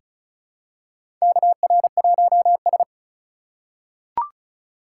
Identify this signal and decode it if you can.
Morse code, heard as a keyed tone, about 36 wpm, 700 Hz. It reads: KR1S
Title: USA Call Signs - 4x Character Spacing - 35wpm